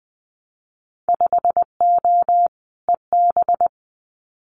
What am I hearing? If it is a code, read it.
Morse code, heard as a keyed tone, about 20 wpm, 700 Hz. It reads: 5O EB
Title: Winter Field Day Exchanges Encore 20wpm